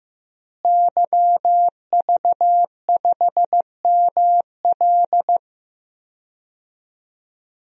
Morse code, heard as a keyed tone, about 15 wpm, 700 Hz. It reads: YV5ML
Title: DXCC-100 Call Signs - 5 Characters - Repeated 3x - 15wpm